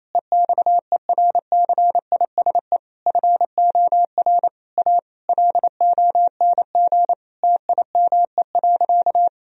Morse code, heard as a keyed tone, about 28 wpm, 700 Hz. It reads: EXERCISE FOR A LONG TIME.
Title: Sentences from Top 1000 Words - Review Part 4 - 28wpm